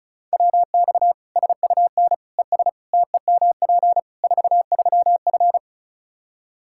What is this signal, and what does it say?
Morse code, heard as a keyed tone, about 34 wpm, 700 Hz. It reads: WX SUN ES TEMP 43F
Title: Practice QSO - Exchange 2 - Info - 35wpm